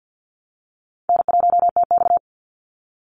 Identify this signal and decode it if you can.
Morse code, heard as a keyed tone, about 48 wpm, 700 Hz. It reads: N1TX
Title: Easy Call Signs Repeated 3x - 50wpm